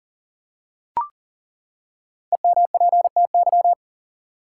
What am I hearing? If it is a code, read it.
Morse code, heard as a keyed tone, about 40 wpm, 700 Hz. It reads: EMPTY
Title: Five Letter Words - 40wpm